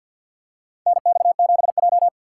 Morse code, heard as a keyed tone, about 53 wpm, 690 Hz. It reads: NX7J
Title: USA Call Signs - 6x Character Spacing - 50wpm